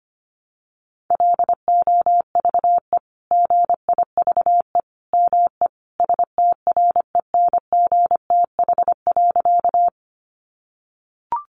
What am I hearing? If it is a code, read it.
Morse code, heard as a keyed tone, about 25 wpm, 700 Hz. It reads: LOVE GIVE ME STRENGTH.
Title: Sentences from Top 1300 Words 25wpm